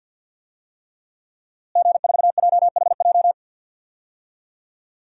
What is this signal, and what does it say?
Morse code, heard as a keyed tone, about 52 wpm, 700 Hz. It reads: G4JHJ
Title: DXCC-100 Call Signs - 5 Characters - Repeated 3x - 50wpm